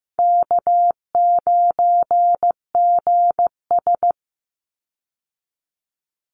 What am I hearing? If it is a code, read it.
Morse code, heard as a keyed tone, about 15 wpm, 700 Hz. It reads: K9GS
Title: Call Signs Repeated 3x - 15wpm